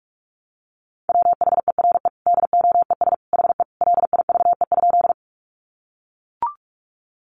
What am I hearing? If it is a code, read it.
Morse code, heard as a keyed tone, about 44 wpm, 700 Hz. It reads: WHERE DOES HE LIVE?
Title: Sentences from Top 200 Words 45wpm